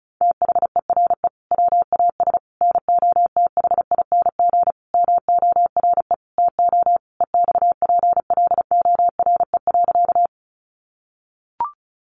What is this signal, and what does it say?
Morse code, heard as a keyed tone, about 34 wpm, 700 Hz. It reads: THERE WAS NOTHING MORE TO EXPLORE.
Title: Sentences from Top 1100 Words 35wpm